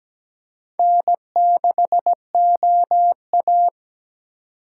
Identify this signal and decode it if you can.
Morse code, heard as a keyed tone, about 20 wpm, 700 Hz. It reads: N6OA
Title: Speed Racing - USA 1x2 Call Signs 17wpm